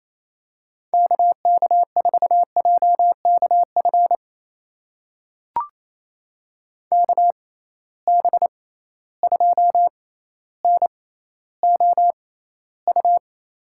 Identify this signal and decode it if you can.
Morse code, heard as a keyed tone, about 28 wpm, spaced as for 14 wpm, 700 Hz. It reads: KK4JKF KB2NOU
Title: USA Call Signs - 6 Characters - 6x Character Spacing - 28wpm